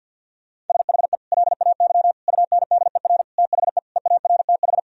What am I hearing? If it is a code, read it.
Morse code, heard as a keyed tone, about 51 wpm, 700 Hz. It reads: SHE LAY UNDER THE EARTH
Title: Sentences from Top 1000 Words - Review Part 2 - 50wpm